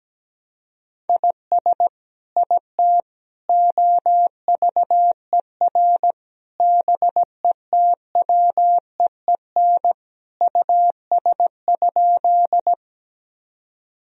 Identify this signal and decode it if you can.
Morse code, heard as a keyed tone, about 17 wpm, 700 Hz. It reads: IS IT OVER BETWEEN US?
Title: Sentences from Top 1000 Words - Review Part 1 - 17wpm